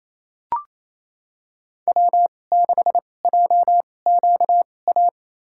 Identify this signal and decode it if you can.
Morse code, heard as a keyed tone, about 28 wpm, spaced as for 20 wpm, 700 Hz. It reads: W6JQA
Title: USA Call Signs - 5 Characters - 2x Character Spacing - 28wpm